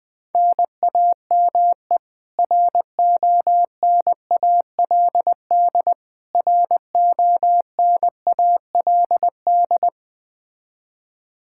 Morse code, heard as a keyed tone, about 20 wpm, 700 Hz. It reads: NAME RONALD RONALD
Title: Practice QSO - Exchange 1 - Beginning - 20wpm